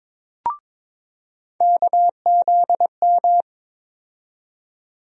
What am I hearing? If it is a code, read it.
Morse code, heard as a keyed tone, about 22 wpm, 700 Hz. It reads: KZM
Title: USA Call Sign Postfixes - 3 Letters 22wpm